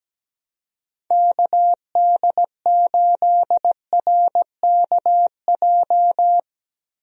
Morse code, heard as a keyed tone, about 17 wpm, 700 Hz. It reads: KD8RKJ
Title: USA Call Signs - 6 Characters - Repeated 3x - 17wpm